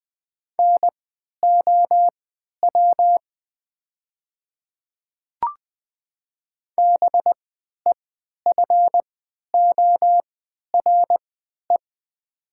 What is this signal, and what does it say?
Morse code, heard as a keyed tone, about 20 wpm, spaced as for 11 wpm, 700 Hz. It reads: NOW BEFORE
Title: 3x Character Spacing - Top 200 Words - 20wpm